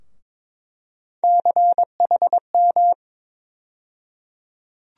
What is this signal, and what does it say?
Morse code, heard as a keyed tone, about 21 wpm, 700 Hz. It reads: CHM
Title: USA Call Sign Postfixes - 3 Letters 22wpm